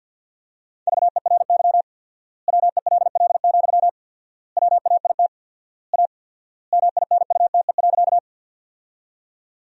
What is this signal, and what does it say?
Morse code, heard as a keyed tone, about 50 wpm, 700 Hz. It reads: VERY WELL, WAIT A MINUTE.
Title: 4x Word-Spacing - Sentences from Top 500 Words - 50wpm